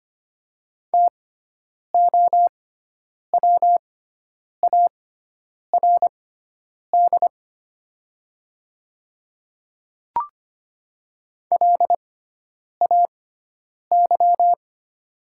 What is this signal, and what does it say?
Morse code, heard as a keyed tone, about 25 wpm, spaced as for 9 wpm, 690 Hz. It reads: TOWARD LAY
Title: 6x Character Spacing - Top 500 Words - 25wpm